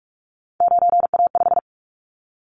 Morse code, heard as a keyed tone, about 44 wpm, 700 Hz. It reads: 9A5
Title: Callsign Prefixes for Top 100 DXCC - 45wpm